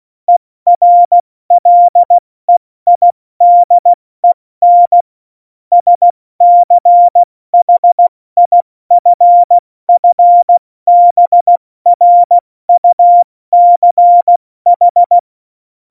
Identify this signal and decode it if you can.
Morse code, heard as a keyed tone, about 16 wpm, 700 Hz. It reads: ERLEIDEN SCHIFFBRUCH
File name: Morsekurs / Witze / Witz 12. Januar 2021, 70 F80